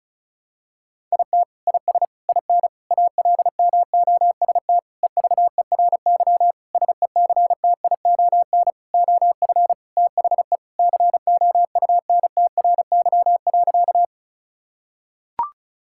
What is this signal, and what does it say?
Morse code, heard as a keyed tone, about 34 wpm, 700 Hz. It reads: IT IS IN ALMOST EVERY SECTION OF THE COUNTRY.
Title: Sentences from Top 1000 Words - Review Part 5 - 35wpm